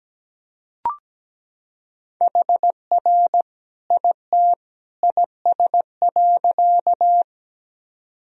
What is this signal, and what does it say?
Morse code, heard as a keyed tone, about 17 wpm, 700 Hz. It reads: HR IT IS.
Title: Sentences from Top 200 Words and Abbreviations 17wpm